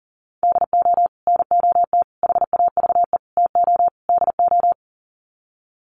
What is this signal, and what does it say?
Morse code, heard as a keyed tone, about 40 wpm, 700 Hz. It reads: DO NOT HAVE TO DO
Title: Sets of 5 Words 40wpm